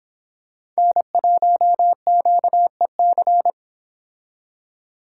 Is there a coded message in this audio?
Yes: Morse code from 0.8 to 3.5 s, heard as a keyed tone, about 26 wpm, 700 Hz. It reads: N1QEC